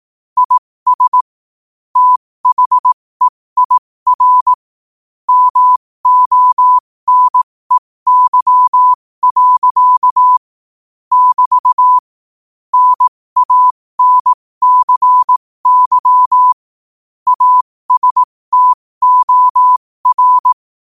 Morse code, heard as a keyed tone, about 18 wpm, spaced as for 15 wpm, 1000 Hz. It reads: IS THEIR MONEY. <BT> NANCY ASTOR
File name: qotdfiles / 15wpm